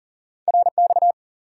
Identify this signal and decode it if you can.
Morse code, heard as a keyed tone, about 40 wpm, 700 Hz. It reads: RX
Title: Top 90 QSO Elements 40wpm